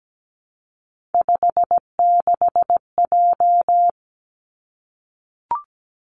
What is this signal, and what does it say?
Morse code, heard as a keyed tone, about 17 wpm, 700 Hz. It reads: 56J
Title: Three Challenging Letter-Number Combinations 17wpm